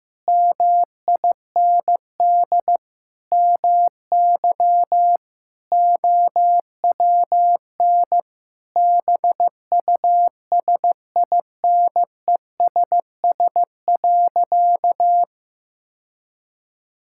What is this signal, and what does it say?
Morse code, heard as a keyed tone, about 16 wpm, 700 Hz. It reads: MIND MY OWN BUSINESS.
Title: Sentences from Top 1000 Words - Review Part 6 - 15wpm